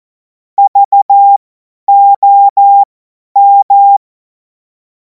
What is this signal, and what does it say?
Morse code, heard as a keyed tone, about 14 wpm, spaced as for 10 wpm, 800 Hz. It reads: VOM
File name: Morsekurs / Witze / Witz 10. März 2026 50 F70